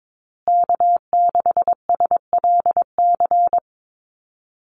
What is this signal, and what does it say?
Morse code, heard as a keyed tone, about 22 wpm, 700 Hz. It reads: K6SLC
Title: USA Call Signs Repeated 2x 22wpm